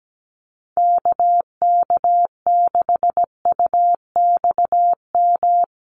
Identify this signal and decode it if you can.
Morse code, heard as a keyed tone, about 17 wpm, 700 Hz. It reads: KK6UXM
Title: USA Call Signs Repeated 2x 17wpm